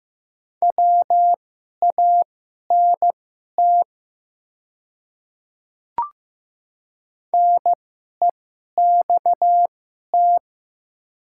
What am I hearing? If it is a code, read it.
Morse code, heard as a keyed tone, about 15 wpm, spaced as for 11 wpm, 700 Hz. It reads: WANT NEXT
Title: 2x Character Spacing - Top 300 Words - 15wpm